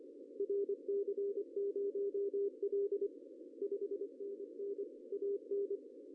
Morse code, heard as a keyed tone, about 23 wpm, 410 Hz, with radio noise. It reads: RC0L 5NN AN
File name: CW / cqww-cw-k5tr